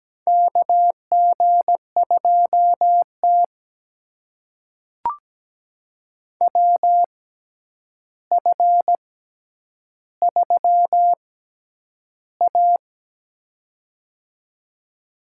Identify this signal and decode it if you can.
Morse code, heard as a keyed tone, about 17 wpm, spaced as for 9 wpm, 700 Hz. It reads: KG2T WF3A
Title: USA Call Signs - 4 Characters - 6x Character Spacing - 17wpm